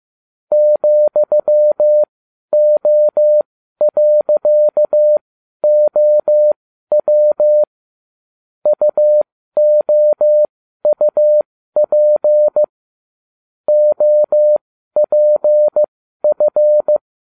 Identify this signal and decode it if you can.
Morse code, heard as a keyed tone, about 15 wpm, spaced as for 12 wpm, 600 Hz. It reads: ,O.OW UOUP OPF
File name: cw / ubacw20 3